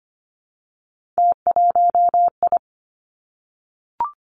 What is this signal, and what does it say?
Morse code, heard as a keyed tone, about 25 wpm, 700 Hz. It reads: T1I
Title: Three Challenging Letter-Number Combinations 25wpm